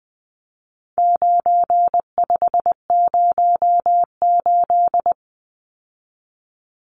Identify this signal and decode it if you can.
Morse code, heard as a keyed tone, about 20 wpm, 700 Hz. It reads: 9508
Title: Numbers - 4-Digit 20wpm